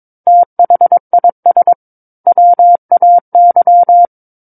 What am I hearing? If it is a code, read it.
Morse code, heard as a keyed tone, about 22 wpm, 700 Hz. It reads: THIS WAY
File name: Book One / 13-Chapter 13 - 22 WPM